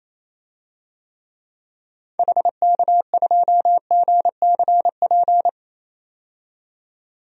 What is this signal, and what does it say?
Morse code, heard as a keyed tone, about 28 wpm, 700 Hz. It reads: HK2GCP
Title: DXCC-100 Call Signs - 6 Characters - 28wpm